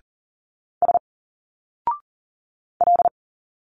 Morse code, heard as a keyed tone, about 39 wpm, 700 Hz. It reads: S L
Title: Single Letters 40wpm